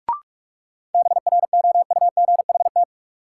Morse code, heard as a keyed tone, about 44 wpm, 700 Hz. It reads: BROUGHT
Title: Sentences from Top 500 Words - New Word Drill - 45wpm